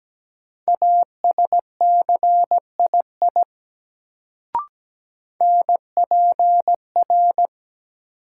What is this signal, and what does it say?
Morse code, heard as a keyed tone, about 17 wpm, 700 Hz. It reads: ASCII NPR
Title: Sets of 5 Words - New Word Drill - 17wpm